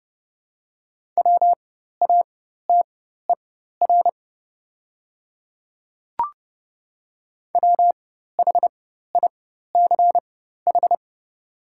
Morse code, heard as a keyed tone, about 29 wpm, spaced as for 14 wpm, 700 Hz. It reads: WATER WHICH
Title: 4x Character Spacing - Top 100 Words - 30wpm